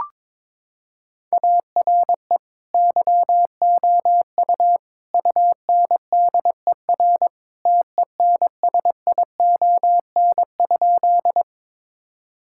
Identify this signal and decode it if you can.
Morse code, heard as a keyed tone, about 21 wpm, 700 Hz. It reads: ARE YOU UNDER TENSION?
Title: Sentences from Top 1600 Words 22wpm